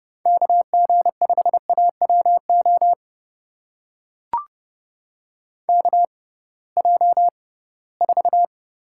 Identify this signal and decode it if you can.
Morse code, heard as a keyed tone, about 30 wpm, spaced as for 19 wpm, 700 Hz. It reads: KG5AWO KJ4
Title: USA Call Signs - 6 Characters - 6x Character Spacing - 30wpm